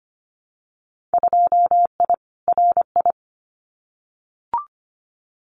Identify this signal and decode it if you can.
Morse code, heard as a keyed tone, about 25 wpm, 700 Hz. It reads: 2I RI
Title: Winter Field Day Exchanges Encore 25wpm